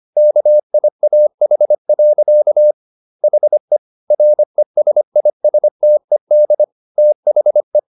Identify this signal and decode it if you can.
Morse code, heard as a keyed tone, about 25 wpm, 600 Hz. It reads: KIAH. HE RESISTED THE